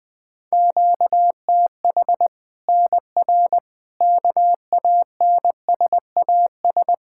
Morse code, heard as a keyed tone, about 21 wpm, 700 Hz. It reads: QTH NR KANSAS
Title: Practice QSO - Exchange 1 - Beginning - 20wpm